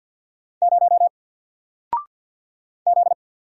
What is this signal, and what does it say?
Morse code, heard as a keyed tone, about 48 wpm, 700 Hz. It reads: 0 Z
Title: Single Letter-Number 50wpm